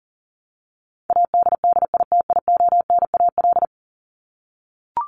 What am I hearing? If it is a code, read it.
Morse code, heard as a keyed tone, about 40 wpm, 700 Hz. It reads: ADDITIONAL
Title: Top 501-750 Words 40wpm